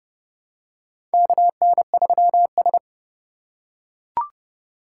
Morse code, heard as a keyed tone, about 31 wpm, 700 Hz. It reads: KN3S
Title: USA Call Signs - 3x Character Spacing - 30wpm